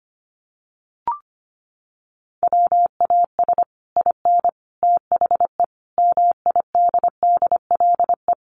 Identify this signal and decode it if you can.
Morse code, heard as a keyed tone, about 25 wpm, 700 Hz. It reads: WAS IN THE MIDDLE
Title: Sets of 5 Words Encore 25wpm